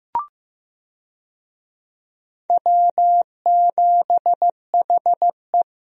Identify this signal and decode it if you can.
Morse code, heard as a keyed tone, about 15 wpm, 700 Hz. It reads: W7HE
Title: USA Call Signs - 4 Characters - 15wpm